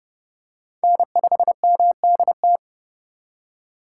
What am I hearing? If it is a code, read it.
Morse code, heard as a keyed tone, about 31 wpm, 700 Hz. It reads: N5MDT
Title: USA Call Signs - 3x Character Spacing - 30wpm